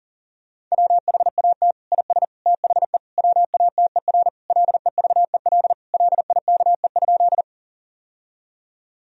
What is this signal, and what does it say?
Morse code, heard as a keyed tone, about 40 wpm, 700 Hz. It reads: WHAT IS THE WATER LEVEL LIKE?
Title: Sentences from Top 1000 Words - Review Part 4 - 40wpm